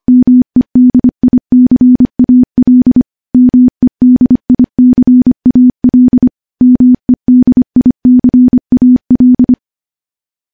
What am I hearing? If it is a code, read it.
Morse code, heard as a keyed tone, about 25 wpm, 250 Hz. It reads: MEDICAL MEDICAL MEDICAL
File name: morse / morse-training-25WPM